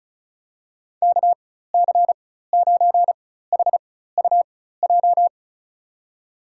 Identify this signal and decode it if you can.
Morse code, heard as a keyed tone, about 36 wpm, spaced as for 16 wpm, 700 Hz. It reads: KC9HUJ
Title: USA Call Signs - 4x Character Spacing - 35wpm